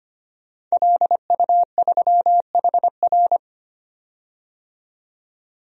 Morse code, heard as a keyed tone, about 25 wpm, 700 Hz. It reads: LU3HR
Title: DXCC-100 Call Signs - 5 Characters - 25wpm